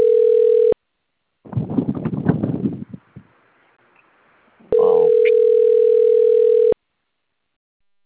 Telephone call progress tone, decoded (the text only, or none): ringback tone